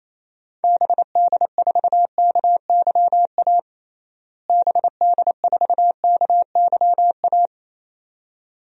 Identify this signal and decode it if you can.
Morse code, heard as a keyed tone, about 28 wpm, 700 Hz. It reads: BD4KYA BD4KYA
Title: DXCC-100 Call Signs - 6 Characters - Repeated 2x - 28wpm